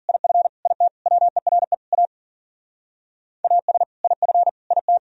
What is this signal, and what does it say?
Morse code, heard as a keyed tone, about 51 wpm, 700 Hz. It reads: IF IT WERE A AS IF IT